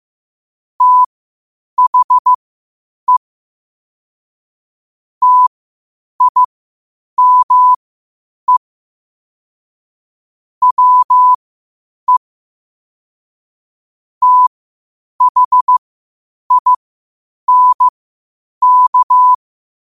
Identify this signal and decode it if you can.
Morse code, heard as a keyed tone, about 15 wpm, spaced as for 8 wpm, 1000 Hz. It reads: THE TIME WE THINK
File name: qotdfiles / 8wpm